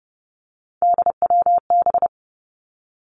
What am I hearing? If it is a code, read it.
Morse code, heard as a keyed tone, about 30 wpm, 700 Hz. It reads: DWB
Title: Three Challenging Letter-Number Combinations 30wpm